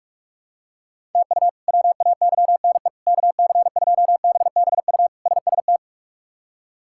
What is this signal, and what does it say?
Morse code, heard as a keyed tone, about 46 wpm, 700 Hz. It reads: TU WAYNE KC2BBU SST
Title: K1USN SST QSO Practice - 45wpm